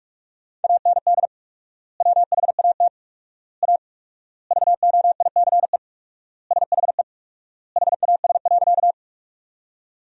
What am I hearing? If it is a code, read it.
Morse code, heard as a keyed tone, about 46 wpm, 700 Hz. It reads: AND WHAT A VOICE SHE HAS.
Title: 4x Word-Spacing - Sentences from Top 500 Words - 45wpm